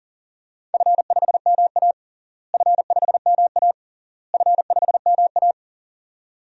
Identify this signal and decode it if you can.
Morse code, heard as a keyed tone, about 37 wpm, 700 Hz. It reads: F5MA F5MA F5MA
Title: DXCC-100 Call Signs - 4 Characters - Repeated 3x - 40wpm